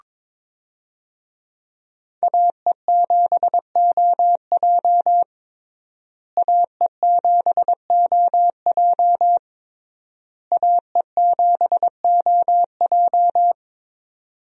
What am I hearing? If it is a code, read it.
Morse code, heard as a keyed tone, about 22 wpm, 700 Hz. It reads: AE7OJ AE7OJ AE7OJ
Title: USA Call Signs - 5 Characters - Repeated 3x - 22wpm